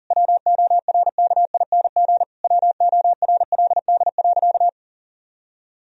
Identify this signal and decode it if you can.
Morse code, heard as a keyed tone, about 37 wpm, 700 Hz. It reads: WORKING WORLD.
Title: Sentences from Top 1000 Words - Review Part 4 - 40wpm